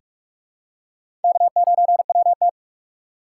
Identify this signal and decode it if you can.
Morse code, heard as a keyed tone, about 42 wpm, 700 Hz. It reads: K9WT
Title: USA Call Signs - 4 Characters - 2x Character Spacing - 45wpm